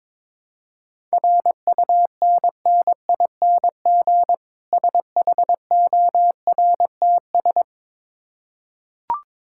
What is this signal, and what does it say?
Morse code, heard as a keyed tone, about 22 wpm, 690 Hz. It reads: RUNNING SHORTS
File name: Difficult Words with Hints - 22wpm